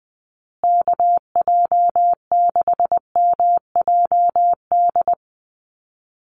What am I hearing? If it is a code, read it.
Morse code, heard as a keyed tone, about 19 wpm, 700 Hz. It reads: KJ6MJD
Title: USA Call Signs Repeated 3x 20wpm